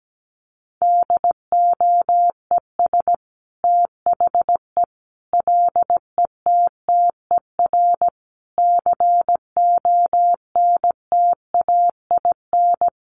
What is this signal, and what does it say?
Morse code, heard as a keyed tone, about 17 wpm, 700 Hz. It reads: DOES THE LETTER CONTAIN
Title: Sentences from Top 1100 Words 17wpm